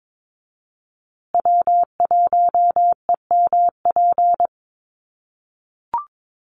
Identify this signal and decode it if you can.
Morse code, heard as a keyed tone, about 22 wpm, 700 Hz. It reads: W1EMP
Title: USA Call Signs Repeated 3x 22wpm